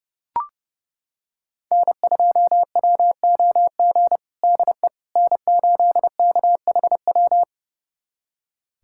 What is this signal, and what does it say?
Morse code, heard as a keyed tone, about 30 wpm, 700 Hz. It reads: N2WOG DE N8KHW
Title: USA Callsign DE Callsign - 5 Characters - 30wpm